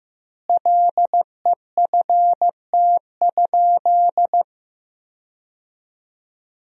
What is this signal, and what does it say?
Morse code, heard as a keyed tone, about 16 wpm, 700 Hz. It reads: LEFT?
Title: 3x Word-Spacing - Sentences from Top 300 Words - 15wpm